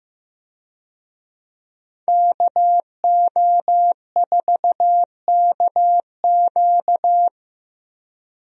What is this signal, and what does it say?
Morse code, heard as a keyed tone, about 15 wpm, 700 Hz. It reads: KO4KQ
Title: USA Call Signs - 5 Characters - Repeated 3x - 15wpm